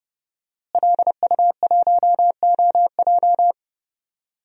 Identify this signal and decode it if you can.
Morse code, heard as a keyed tone, about 30 wpm, 700 Hz. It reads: LU1OJ
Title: DXCC-100 Call Signs - 5 Characters - Repeated 2x - 30wpm